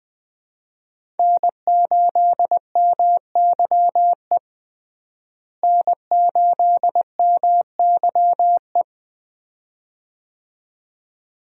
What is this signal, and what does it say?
Morse code, heard as a keyed tone, about 20 wpm, 700 Hz. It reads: N8MYE N8MYE
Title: USA Call Signs - 5 Characters - Repeated 2x - 20wpm